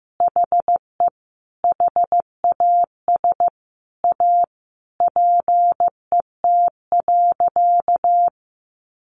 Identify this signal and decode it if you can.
Morse code, heard as a keyed tone, about 15 wpm, 700 Hz. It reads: HE HAS A PET.
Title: Easy Phrases 15wpm